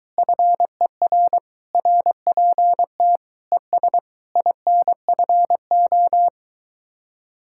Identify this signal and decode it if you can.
Morse code, heard as a keyed tone, about 22 wpm, 700 Hz. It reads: FER RPT ES INFO